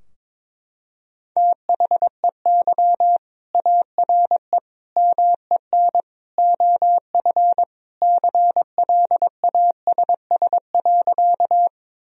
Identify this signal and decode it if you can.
Morse code, heard as a keyed tone, about 22 wpm, 700 Hz. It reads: THEY ARE MEN OF CLASS.
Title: Sentences from Top 1000 Words - Review Part 2 - 22wpm